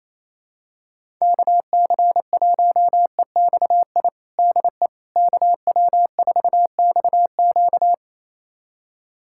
Morse code, heard as a keyed tone, about 28 wpm, 700 Hz. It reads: KC1EXI DE KW4XQ
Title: USA Callsign DE Callsign 28wpm